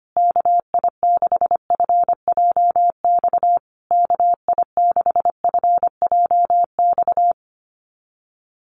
Morse code, heard as a keyed tone, about 25 wpm, 700 Hz. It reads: KI6FJX KI6FJX
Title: USA Call Signs Repeated 3x 25wpm